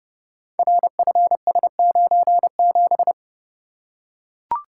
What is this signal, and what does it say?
Morse code, heard as a keyed tone, about 30 wpm, 700 Hz. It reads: RFS97
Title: Code Groups - 5 Letter:Numbers - 30wpm